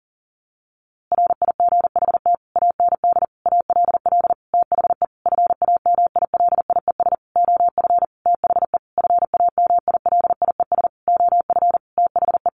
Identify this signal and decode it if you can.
Morse code, heard as a keyed tone, about 40 wpm, 700 Hz. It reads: RIGHT AND ALL THE FAMILIES OF THE FAMILIES OF THE
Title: A Sound Of Thunder 40wpm